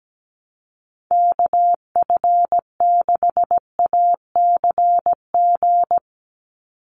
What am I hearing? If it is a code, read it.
Morse code, heard as a keyed tone, about 17 wpm, 700 Hz. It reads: KF6ACG
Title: USA Call Signs 17wpm